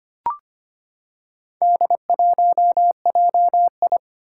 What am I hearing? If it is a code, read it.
Morse code, heard as a keyed tone, about 25 wpm, 700 Hz. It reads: D1JI
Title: Four Challenging Letter-Number Combinations 25wpm